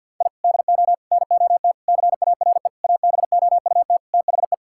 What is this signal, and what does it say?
Morse code, heard as a keyed tone, about 50 wpm, 700 Hz. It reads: I DO NOT CARE ABOUT THE
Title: Sentences from Top 1000 Words - Review Part 5 - 50wpm